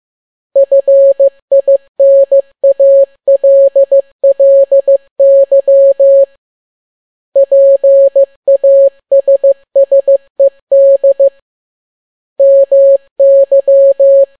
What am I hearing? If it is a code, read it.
Morse code, heard as a keyed tone, about 15 wpm, 550 Hz. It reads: FINALLY PASSED MY
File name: n6tt p1 15C13